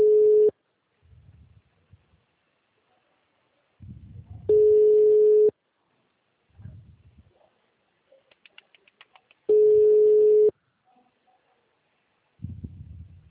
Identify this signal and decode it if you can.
Ringback tone (Europe, 425 Hz).
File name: out- -2019- - - .13870